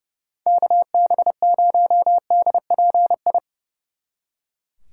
Morse code, heard as a keyed tone, about 28 wpm, 700 Hz. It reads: KB0DPI
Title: Speed Racing - USA 2x3 Call Signs 30wpm